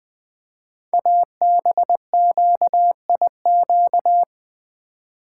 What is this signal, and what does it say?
Morse code, heard as a keyed tone, about 20 wpm, 700 Hz. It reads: ABQIQ
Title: Code Groups - 5 Letters - 20wpm